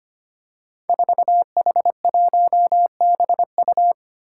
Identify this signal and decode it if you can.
Morse code, heard as a keyed tone, about 25 wpm, 700 Hz. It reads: 4H1BU
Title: Five Challenging Letter-Number Combinations 25wpm